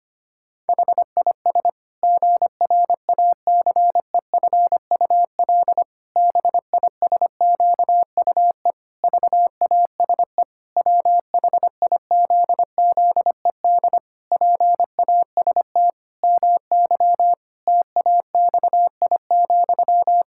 Morse code, heard as a keyed tone, about 25 wpm, 700 Hz. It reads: HIS GRACEFUL BISQUE VASE WHIZZED PAST MY TAXI,
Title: Pangram Challenge v2 - 25wpm